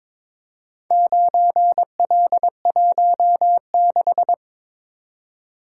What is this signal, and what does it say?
Morse code, heard as a keyed tone, about 21 wpm, 700 Hz. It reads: 9L16
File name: practice / 22wpm / Four Letter-Number Combinations - 22wpm